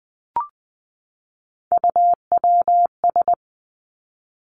Morse code, heard as a keyed tone, about 19 wpm, 700 Hz. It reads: UWS
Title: Three Challenging Letter-Number Combinations 20wpm